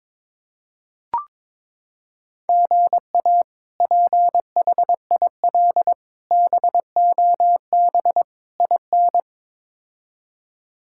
Morse code, heard as a keyed tone, about 22 wpm, 700 Hz. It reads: GA PHIL BOB IN
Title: K1USN SST QSO Practice - 22wpm